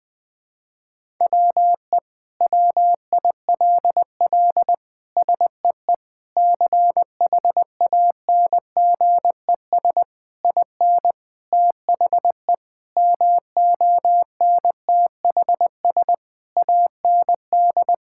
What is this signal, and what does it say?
Morse code, heard as a keyed tone, about 20 wpm, 700 Hz. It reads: WE WILL SEE CHANGES IN THE MONTHS AND
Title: Sentences with 7-Word Phrases - 20wpm